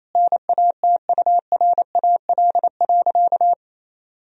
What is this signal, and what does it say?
Morse code, heard as a keyed tone, about 28 wpm, 700 Hz. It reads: NATURAL.
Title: Sentences from Top 1000 Words - Review Part 4 - 28wpm